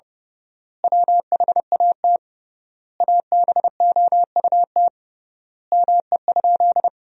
Morse code, heard as a keyed tone, about 32 wpm, 700 Hz. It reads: WHAT ABOUT ME?
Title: 3x Word-Spacing - Sentences from Top 200 Words - 30wpm